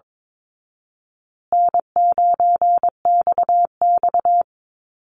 Morse code, heard as a keyed tone, about 22 wpm, 700 Hz. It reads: N9XX
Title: Call Signs Repeated 2x - 22wpm